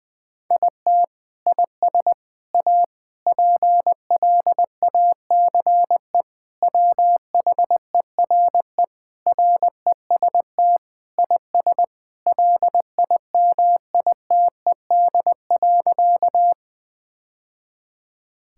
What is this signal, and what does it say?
Morse code, heard as a keyed tone, about 21 wpm, 700 Hz. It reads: IT IS A PLACE WHERE REST IS LIMITED.
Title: Sentences from Top 1000 Words - Review Part 6 - 20wpm